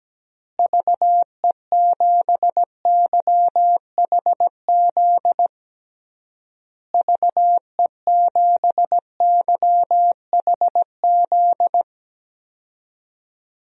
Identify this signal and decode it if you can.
Morse code, heard as a keyed tone, about 17 wpm, 700 Hz. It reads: VE7YHZ VE7YHZ
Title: DXCC-100 Call Signs - 6 Characters - Repeated 3x - 17wpm